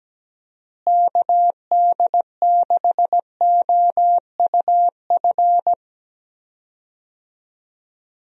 Morse code, heard as a keyed tone, about 17 wpm, 700 Hz. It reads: KD6OUF
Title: USA Call Signs - 6 Characters - Repeated 3x - 17wpm